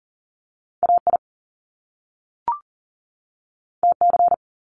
Morse code, heard as a keyed tone, about 38 wpm, 700 Hz. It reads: AI TC
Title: Top Two Letter Combinations in Words 40wpm